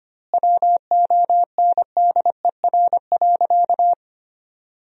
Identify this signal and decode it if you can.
Morse code, heard as a keyed tone, about 25 wpm, 700 Hz. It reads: WONDER.
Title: 2x Word-Spacing - Sentences from Top 500 Words - 25wpm